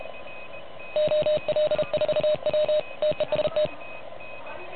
Morse code, heard as a keyed tone, about 34 wpm, 590 Hz. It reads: OL4W TEST
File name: truSDX / MWC29.8